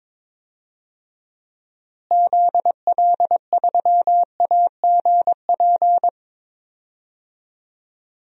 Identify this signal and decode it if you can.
Morse code, heard as a keyed tone, about 23 wpm, 700 Hz. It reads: ZL3AGP
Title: DXCC-100 Call Signs - 6 Characters - Repeated 2x - 22wpm